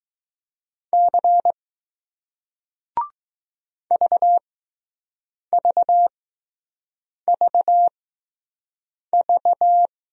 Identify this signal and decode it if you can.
Morse code, heard as a keyed tone, about 20 wpm, spaced as for 6 wpm, 700 Hz. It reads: C VVVV